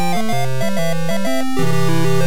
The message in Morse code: none